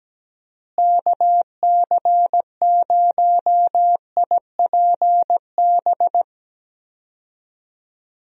Morse code, heard as a keyed tone, about 17 wpm, 700 Hz. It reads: KC0IPB
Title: USA Call Signs - 6 Characters - Repeated 2x - 17wpm